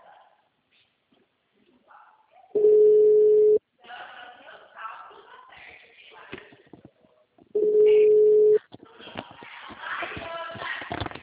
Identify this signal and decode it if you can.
Ringback tone (Europe, 425 Hz).
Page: Itapetininga